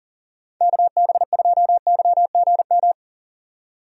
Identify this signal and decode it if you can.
Morse code, heard as a keyed tone, about 38 wpm, 700 Hz. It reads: KB2YGM
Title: USA Call Signs - 6 Characters - 40wpm